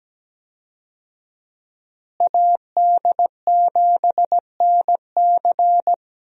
Morse code, heard as a keyed tone, about 17 wpm, 700 Hz. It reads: AD7NC
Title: USA Call Signs - 5 Characters - Repeated 3x - 17wpm